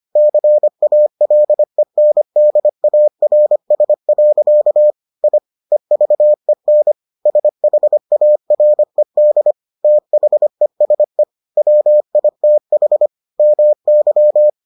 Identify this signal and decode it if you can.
Morse code, heard as a keyed tone, about 25 wpm, 600 Hz. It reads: CALENDARS. I EVEN SHARED THESE WITH MY